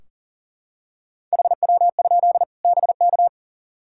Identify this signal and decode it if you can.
Morse code, heard as a keyed tone, about 39 wpm, 700 Hz. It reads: HW? BK
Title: Practice QSO - Exchange 1 - Beginning - 40wpm